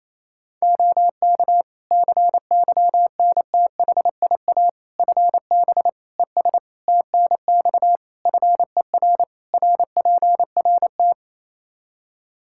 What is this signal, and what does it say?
Morse code, heard as a keyed tone, about 27 wpm, 700 Hz. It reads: OK CYNTHIA FB ES TNX FER RPRT
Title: K1USN SST QSO Practice 28wpm